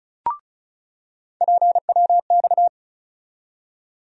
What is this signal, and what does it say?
Morse code, heard as a keyed tone, about 34 wpm, 700 Hz. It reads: PWX
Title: USA Call Sign Postfixes - 3 Letters 35wpm